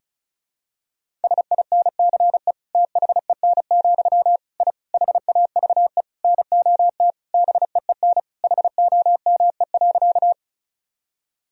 Morse code, heard as a keyed tone, about 36 wpm, 700 Hz. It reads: SINCE THEN, I HAVE NOT BEEN HOME.
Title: Sentences from Top 1000 Words - Review Part 1 - 35wpm